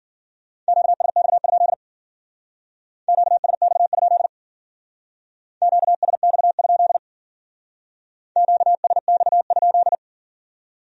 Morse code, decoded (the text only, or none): QSX? QSX? QSX? QSX?